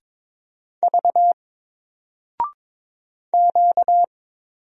Morse code, heard as a keyed tone, about 22 wpm, 700 Hz. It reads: V Q
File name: Single-Letter-Number-22wpm